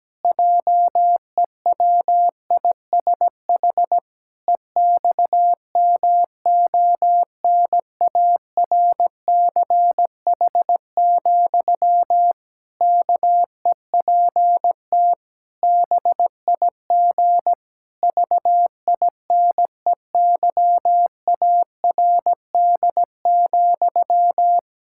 Morse code, heard as a keyed tone, about 17 wpm, 700 Hz. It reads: JEWISH EXMONARCH, KEPT BIG VINEYARD,
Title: Pangram Challenge v2 - 17wpm